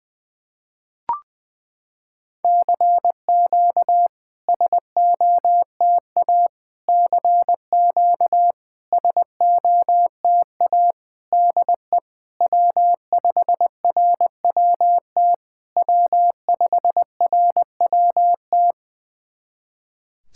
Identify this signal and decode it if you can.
Morse code, heard as a keyed tone, about 20 wpm, 700 Hz. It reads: CQ SOTA CQ SOTA DE W5RWT W5RWT
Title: SOTA QSO Practice 20wpm